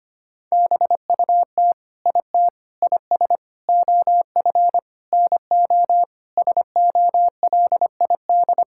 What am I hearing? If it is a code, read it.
Morse code, heard as a keyed tone, about 25 wpm, 700 Hz. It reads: BUT IT IS OF NO SOLID
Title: Sentences from Top 1600 Words 25wpm